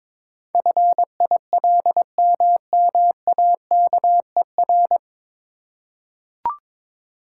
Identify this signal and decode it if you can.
Morse code, heard as a keyed tone, about 22 wpm, 700 Hz. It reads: FILMMAKER
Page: Morse Code Ninja practice 22wpm